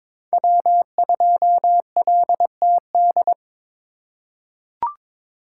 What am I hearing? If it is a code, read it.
Morse code, heard as a keyed tone, about 22 wpm, 700 Hz. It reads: W2LTD
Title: USA Call Signs - 5 Characters - 4x Character Spacing - 22wpm